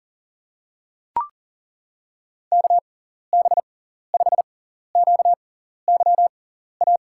Morse code, heard as a keyed tone, about 40 wpm, spaced as for 14 wpm, 700 Hz. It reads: KB5QYA
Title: USA Call Signs - 6 Characters - 6x Character Spacing - 40wpm